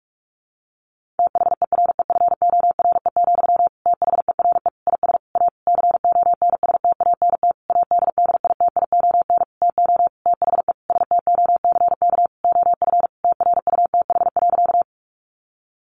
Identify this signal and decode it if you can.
Morse code, heard as a keyed tone, about 44 wpm, 700 Hz. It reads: THEREFORE, THERE IS A CONSTANT ADDITION TO THE STOCK OF TRUTH.
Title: Sentences from Top 1500 Words 45wpm